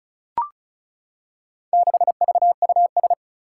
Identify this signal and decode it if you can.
Morse code, heard as a keyed tone, about 36 wpm, 700 Hz. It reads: 6VUS
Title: Four Challenging Letter-Number Combinations 35wpm